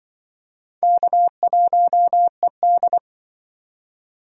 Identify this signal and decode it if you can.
Morse code, heard as a keyed tone, about 26 wpm, 700 Hz. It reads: K1ED